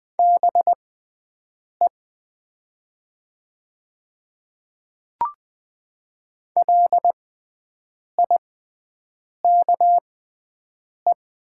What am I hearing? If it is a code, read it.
Morse code, heard as a keyed tone, about 20 wpm, spaced as for 7 wpm, 690 Hz. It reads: BE LIKE
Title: 6x Character Spacing - Top 100 Words - 20wpm